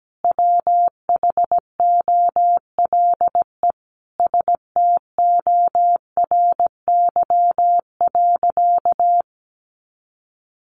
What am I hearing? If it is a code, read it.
Morse code, heard as a keyed tone, about 17 wpm, 700 Hz. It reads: WHOLE STORY.
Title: Sentences from Top 1100 Words 17wpm